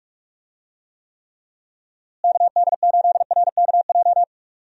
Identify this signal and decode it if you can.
Morse code, heard as a keyed tone, about 46 wpm, 700 Hz. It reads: KD8RKJ
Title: USA Call Signs - 6 Characters - Repeated 3x - 45wpm